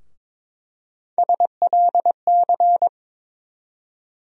Morse code, heard as a keyed tone, about 22 wpm, 700 Hz. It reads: SLC
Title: USA Call Sign Postfixes - 3 Letters 22wpm